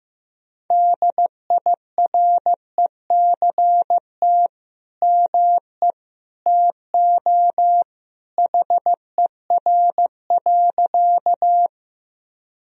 Morse code, heard as a keyed tone, about 15 wpm, 700 Hz. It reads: DIRECT ME TO HER.
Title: Sentences from Top 1000 Words - Review Part 2 - 15wpm